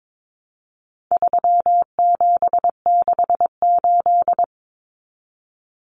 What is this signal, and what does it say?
Morse code, heard as a keyed tone, about 22 wpm, 700 Hz. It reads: 3768
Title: Numbers - 4-Digit 22wpm